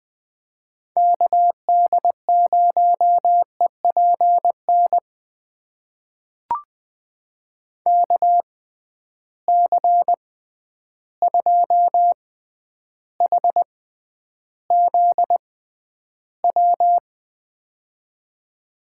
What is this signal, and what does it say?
Morse code, heard as a keyed tone, about 20 wpm, spaced as for 10 wpm, 700 Hz. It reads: KD0EPN KC2HZW
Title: USA Call Signs - 6 Characters - 6x Character Spacing - 20wpm